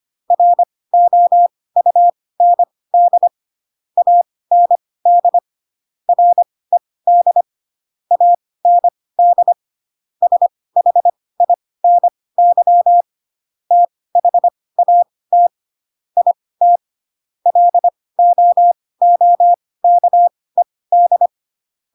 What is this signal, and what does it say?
Morse code, heard as a keyed tone, about 25 wpm, spaced as for 18 wpm, 700 Hz. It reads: ROUND AND RED AND SHINY THAT IT LOOKED
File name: Pinocchio / Pinocchio ch1 12 25 ik1hge